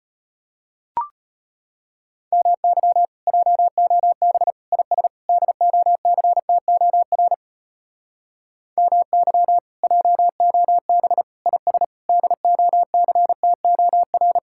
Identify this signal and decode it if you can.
Morse code, heard as a keyed tone, about 33 wpm, 700 Hz. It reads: MY JOB IS DOCTOR MY JOB IS DOCTOR